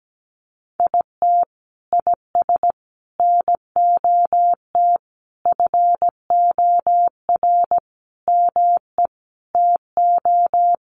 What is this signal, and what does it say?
Morse code, heard as a keyed tone, about 17 wpm, 700 Hz. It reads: IT IS NOT FOR ME TO
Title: Sentences from Top 1500 Words 17wpm